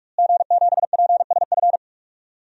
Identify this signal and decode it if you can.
Morse code, heard as a keyed tone, about 46 wpm, 700 Hz. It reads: G7PSF
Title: DXCC-100 Call Signs - 5 Characters - Repeated 3x - 45wpm